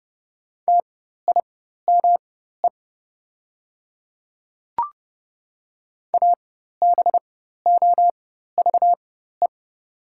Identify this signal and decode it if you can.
Morse code, heard as a keyed tone, about 29 wpm, spaced as for 14 wpm, 690 Hz. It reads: TIME ABOVE
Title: 4x Character Spacing - Top 500 Words - 30wpm